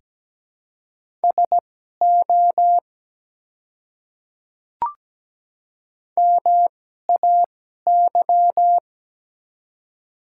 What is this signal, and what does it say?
Morse code, heard as a keyed tone, about 17 wpm, spaced as for 12 wpm, 700 Hz. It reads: SO MAY
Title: 2x Character Spacing - Top 200 Words - 17wpm